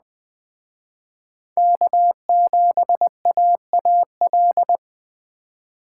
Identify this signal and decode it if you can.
Morse code, heard as a keyed tone, about 21 wpm, 700 Hz. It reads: K7AAL